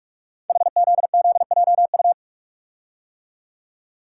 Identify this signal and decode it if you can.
Morse code, heard as a keyed tone, about 44 wpm, 700 Hz. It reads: HZZJU
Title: Code Groups - 5 Letters - 45wpm